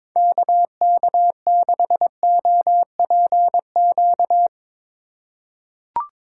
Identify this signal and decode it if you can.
Morse code, heard as a keyed tone, about 21 wpm, 700 Hz. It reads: KK6OPQ
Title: USA Call Signs - 2x Character Spacing - 22wpm